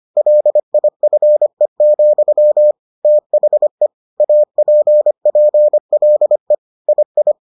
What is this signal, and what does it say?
Morse code, heard as a keyed tone, about 25 wpm, 600 Hz. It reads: LIFE, THE APPLE II